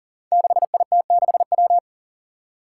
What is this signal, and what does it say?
Morse code, heard as a keyed tone, about 37 wpm, 700 Hz. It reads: 6IT6W
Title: Five Challenging Letter-Number Combinations 40wpm